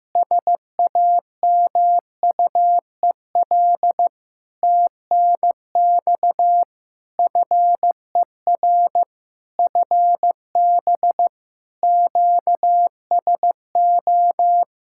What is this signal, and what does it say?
Morse code, heard as a keyed tone, about 15 wpm, 700 Hz. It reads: SAMUEL TNX FER FB QSO
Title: K1USN SST QSO Practice 15wpm